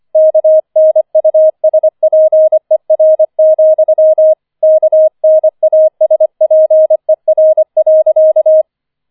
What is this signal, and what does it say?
Morse code, heard as a keyed tone, about 25 wpm, 630 Hz. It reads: KNUSPER, KNASPER.
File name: 25wpm